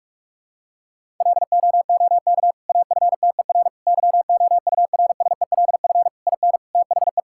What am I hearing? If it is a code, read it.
Morse code, heard as a keyed tone, about 46 wpm, 700 Hz. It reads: LOOK AFTER YOURSELF IN THE
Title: Sentences from Top 1000 Words - Review Part 6 - 45wpm